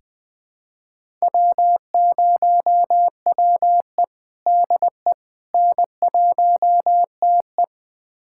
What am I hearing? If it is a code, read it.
Morse code, heard as a keyed tone, about 20 wpm, 700 Hz. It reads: W0WE DE N1TE
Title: USA Callsign DE Callsign - 4 Characters - 20wpm